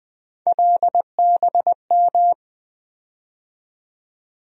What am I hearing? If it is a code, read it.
Morse code, heard as a keyed tone, about 20 wpm, 700 Hz. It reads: LBM
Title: USA Call Sign Postfixes - 3 Letters 20wpm